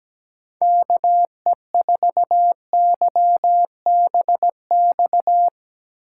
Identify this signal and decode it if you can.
Morse code, heard as a keyed tone, about 17 wpm, 700 Hz. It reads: KE4YBX
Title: USA Call Signs - 8x Character Spacing - 17wpm